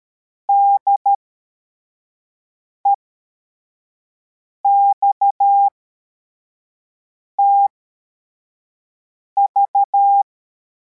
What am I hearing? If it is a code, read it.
Morse code, heard as a keyed tone, about 13 wpm, spaced as for 4 wpm, 800 Hz. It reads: DEXTV